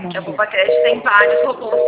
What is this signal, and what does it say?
Reorder tone (North America), 480 + 620 Hz, with speech.
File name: out- -2003- - - .10893